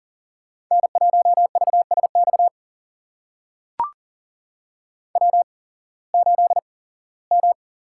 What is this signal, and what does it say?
Morse code, heard as a keyed tone, about 40 wpm, spaced as for 21 wpm, 700 Hz. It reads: N1VSX W8M
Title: USA Call Signs - 5 Characters - 8x Character Spacing - 40wpm